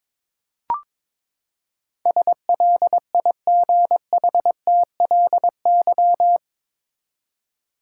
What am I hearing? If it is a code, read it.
Morse code, heard as a keyed tone, about 22 wpm, 700 Hz. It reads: SLIGHTLY
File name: practice / 22wpm / Eight Letter Words - 22wpm